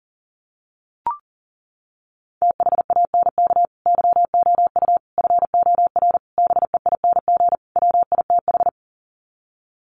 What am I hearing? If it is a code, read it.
Morse code, heard as a keyed tone, about 40 wpm, 700 Hz. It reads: THANK YOU FOR BEING WITH
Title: Sets of 5 Words 40wpm